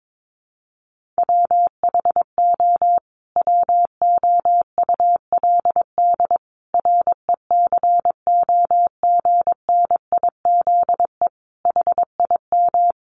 Morse code, heard as a keyed tone, about 22 wpm, 700 Hz. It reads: WHO WOULD RECOGNIZE HIM
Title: Sentences from Top 1400 Words 22wpm